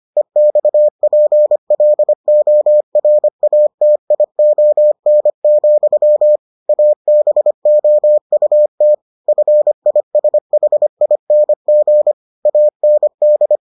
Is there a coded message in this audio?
Yes: Morse code from 0.2 to 13.6 s, heard as a keyed tone, about 25 wpm, 600 Hz. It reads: EXPLORATION, ABOUT FISHING AND